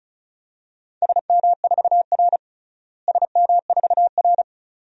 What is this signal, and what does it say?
Morse code, heard as a keyed tone, about 35 wpm, 700 Hz. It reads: SM4R SM4R
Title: DXCC-100 Call Signs - 4 Characters - Repeated 2x - 35wpm